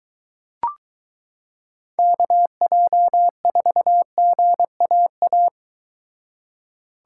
Morse code, heard as a keyed tone, about 22 wpm, 700 Hz. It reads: KJ4GAA